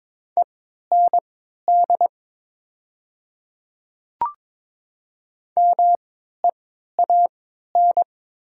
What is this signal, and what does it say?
Morse code, heard as a keyed tone, about 21 wpm, spaced as for 12 wpm, 700 Hz. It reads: END MEAN
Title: 3x Character Spacing - Top 200 Words - 22wpm